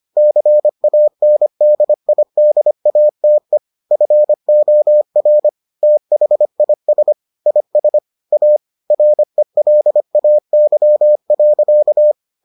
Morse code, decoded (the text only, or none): CANDIDATE FOR THIS IS A RELAY.